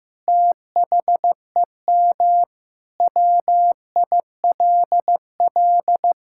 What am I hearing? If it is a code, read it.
Morse code, heard as a keyed tone, about 15 wpm, 700 Hz. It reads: THEM WILL
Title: Sentences from Top 1600 Words 15wpm